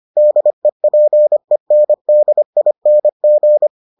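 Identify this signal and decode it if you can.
Morse code, heard as a keyed tone, about 25 wpm, 600 Hz. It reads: DEPENDING
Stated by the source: Morse edition of the Foundations of Amateur Radio episodes.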